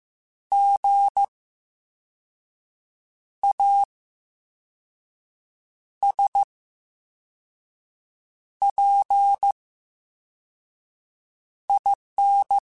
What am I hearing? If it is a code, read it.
Morse code, heard as a keyed tone, about 15 wpm, spaced as for 4 wpm, 800 Hz. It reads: GASPIN
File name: Special INGs-mp3s / Special INGs-mp3s / VerySlowToFaster Ing @13